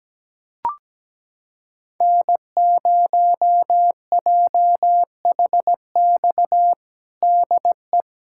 Morse code, heard as a keyed tone, about 17 wpm, 700 Hz. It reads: N0JHX DE
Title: USA Callsign DE Callsign - 5 Characters - 17wpm